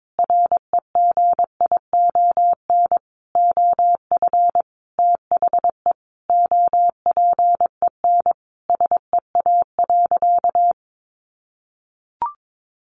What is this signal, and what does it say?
Morse code, heard as a keyed tone, about 22 wpm, 700 Hz. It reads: REGION OF THE OPEN SEA.
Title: Sentences from Top 1500 Words 22wpm